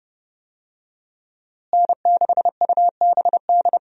Morse code, heard as a keyed tone, about 30 wpm, 700 Hz. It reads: N6UBD
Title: USA Call Signs - 5 Characters - Repeated 3x - 30wpm